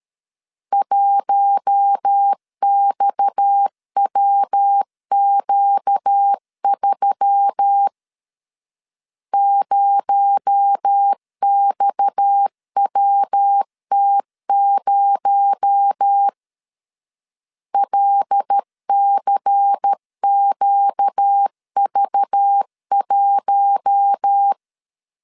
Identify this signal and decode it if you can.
Morse code, heard as a keyed tone, about 13 wpm, 800 Hz. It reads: 1XWQ3 0XWT0 LCQV1
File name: LEZIONI / 6 PKX0 / 6 PKX0 12wpm 2231